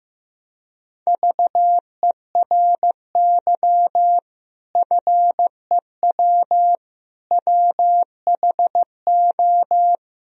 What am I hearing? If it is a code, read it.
Morse code, heard as a keyed tone, about 15 wpm, 700 Hz. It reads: VERY FEW WHO
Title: Sentences from Top 1000 Words - Review Part 2 - 15wpm